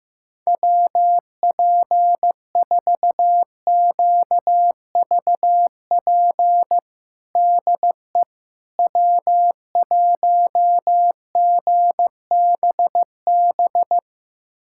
Morse code, heard as a keyed tone, about 16 wpm, 700 Hz. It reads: WP4QVP DE W1GBB
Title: USA Callsign DE Callsign 15wpm